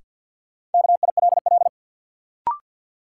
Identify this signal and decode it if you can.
Morse code, heard as a keyed tone, about 52 wpm, 700 Hz. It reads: KILL
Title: Sentences from Top 1400 Words - New Word Drill Alt - 50wpm